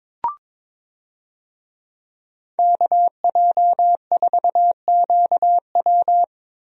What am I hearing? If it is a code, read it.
Morse code, heard as a keyed tone, about 22 wpm, 700 Hz. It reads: KJ4QW
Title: USA Call Signs - 5 Characters - Repeated 3x - 22wpm